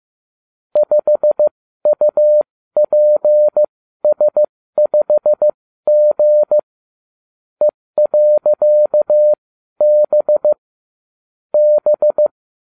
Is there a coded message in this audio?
Yes: Morse code from 0.8 to 12.3 s, heard as a keyed tone, about 15 wpm, spaced as for 12 wpm, 600 Hz. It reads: 5UPS5G E.B B